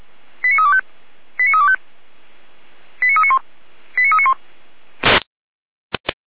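Five-tone selective call (ZVEI), heard as quick runs of five tones, tones of about 70 ms. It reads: 89325 89325 89391 89391
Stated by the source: Five-tone ZVEI selective call received by SDR.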